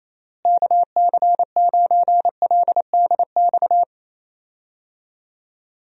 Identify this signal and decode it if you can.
Morse code, heard as a keyed tone, about 28 wpm, 700 Hz. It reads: KC9LDX
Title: USA Call Signs - 6 Characters - Repeated 3x - 28wpm